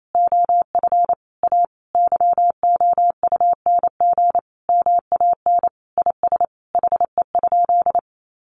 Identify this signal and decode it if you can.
Morse code, heard as a keyed tone, about 28 wpm, 700 Hz. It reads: OF A YOUNG MAN IS HE?
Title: Sentences from Top 1300 Words 28wpm